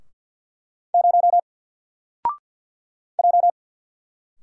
Morse code, heard as a keyed tone, about 52 wpm, 700 Hz. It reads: 0 J
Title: Single Letter-Number Alt 50wpm